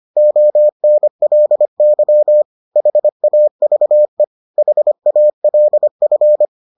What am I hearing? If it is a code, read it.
Morse code, heard as a keyed tone, about 25 wpm, 600 Hz. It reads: ONLY HAVE HALF